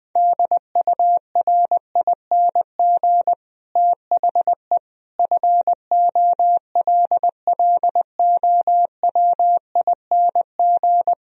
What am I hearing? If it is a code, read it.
Morse code, heard as a keyed tone, about 19 wpm, 700 Hz. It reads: DURING THE FOLLOWING
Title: Sentences from Top 1000 Words - Review Part 4 - 20wpm